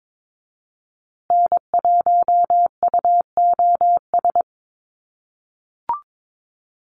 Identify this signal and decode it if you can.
Morse code, heard as a keyed tone, about 22 wpm, 700 Hz. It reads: N1UOS
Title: USA Call Signs Repeated 3x 22wpm